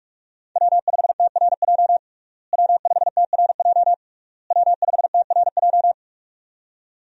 Morse code, heard as a keyed tone, about 41 wpm, 700 Hz. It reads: W5TRJ W5TRJ W5TRJ
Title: USA Call Signs - 5 Characters - Repeated 3x - 45wpm